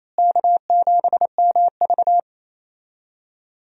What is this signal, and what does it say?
Morse code, heard as a keyed tone, about 28 wpm, 700 Hz. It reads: K7MV
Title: USA Call Signs - 4 Characters - Repeated 2x - 28wpm